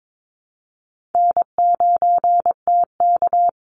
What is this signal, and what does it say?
Morse code, heard as a keyed tone, about 22 wpm, 700 Hz. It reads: N9TK
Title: Easy Call Signs Repeated 2x - 22wpm